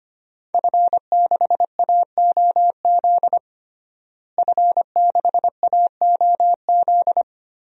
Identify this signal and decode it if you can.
Morse code, heard as a keyed tone, about 25 wpm, 700 Hz. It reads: F6AOZ F6AOZ
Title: DXCC-100 Call Signs - 5 Characters - Repeated 3x - 25wpm